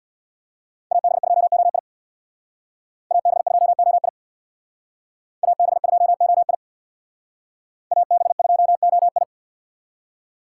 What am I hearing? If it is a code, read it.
Morse code, heard as a keyed tone, about 62 wpm, 700 Hz. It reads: AB2OI AB2OI AB2OI AB2OI
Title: Speed Racing - USA 2x2 Call Signs 50wpm